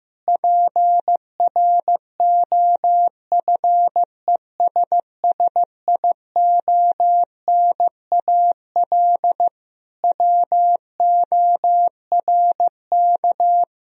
Morse code, heard as a keyed tone, about 15 wpm, 700 Hz. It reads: PROFESSIONAL WORK
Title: Sentences from Top 1000 Words - Review Part 4 - 15wpm